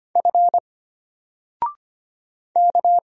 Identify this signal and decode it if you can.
Morse code, heard as a keyed tone, about 25 wpm, 700 Hz. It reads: F K
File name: Single-Letter-Number-25wpm